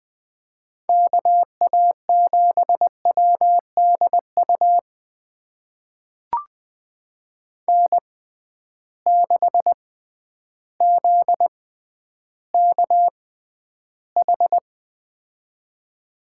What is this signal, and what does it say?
Morse code, heard as a keyed tone, about 20 wpm, spaced as for 11 wpm, 700 Hz. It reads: KA7WDU N6ZKH
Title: USA Call Signs - 6x Character Spacing - 20wpm